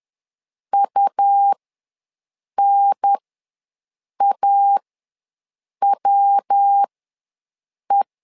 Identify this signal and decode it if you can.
Morse code, heard as a keyed tone, about 10 wpm, spaced as for 6 wpm, 800 Hz. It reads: UNAWE
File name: LEZIONI / 3 MWRO / 3 MW ws 10wpm 2213